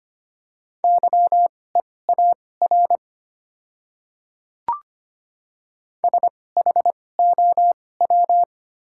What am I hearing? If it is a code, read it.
Morse code, heard as a keyed tone, about 25 wpm, spaced as for 18 wpm, 700 Hz. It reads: YEAR SHOW
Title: 2x Character Spacing - Top 200 Words - 25wpm